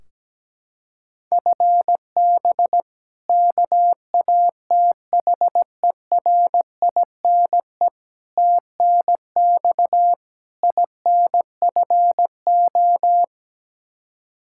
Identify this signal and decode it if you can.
Morse code, heard as a keyed tone, about 17 wpm, 700 Hz. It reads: FB KATHERINE TNX INFO
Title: K1USN SST QSO Practice 17wpm